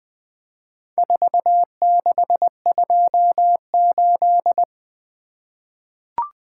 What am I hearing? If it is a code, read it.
Morse code, heard as a keyed tone, about 20 wpm, 700 Hz. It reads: 4628
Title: Four Letter-Number Combinations - 20wpm